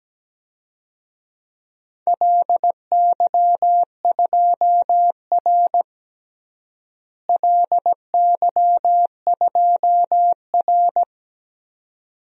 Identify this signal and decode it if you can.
Morse code, heard as a keyed tone, about 17 wpm, 700 Hz. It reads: LY2R LY2R
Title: DXCC-100 Call Signs - 4 Characters - Repeated 2x - 17wpm